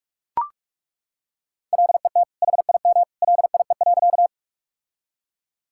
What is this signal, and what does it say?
Morse code, heard as a keyed tone, about 41 wpm, 700 Hz. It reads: LET HIM LIE.
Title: Sentences from Top 1600 Words 45wpm